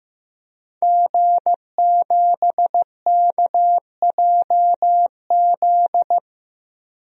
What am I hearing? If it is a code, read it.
Morse code, heard as a keyed tone, about 15 wpm, 700 Hz. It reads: G7KJZ
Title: DXCC-100 Call Signs - 5 Characters - 15wpm